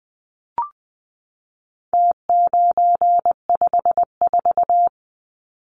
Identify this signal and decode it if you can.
Morse code, heard as a keyed tone, about 20 wpm, 700 Hz. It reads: T954
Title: Numbers - Contest Style - 4-Digit 20wpm